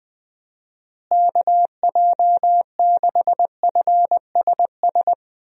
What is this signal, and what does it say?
Morse code, heard as a keyed tone, about 20 wpm, 700 Hz. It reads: KJ6FSS
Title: USA Call Signs - 6 Characters - Repeated 2x - 20wpm